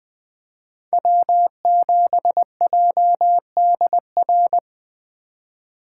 Morse code, heard as a keyed tone, about 20 wpm, 700 Hz. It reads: W7JDR
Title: Speed Racing - USA 1x3 Call Signs 17wpm